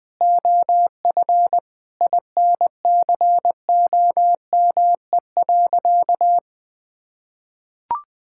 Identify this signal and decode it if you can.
Morse code, heard as a keyed tone, about 21 wpm, 700 Hz. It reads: OF INCOME.
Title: Sentences from Top 1000 Words - Review Part 5 - 20wpm